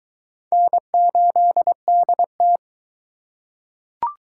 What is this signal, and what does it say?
Morse code, heard as a keyed tone, about 21 wpm, 700 Hz. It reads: N8DT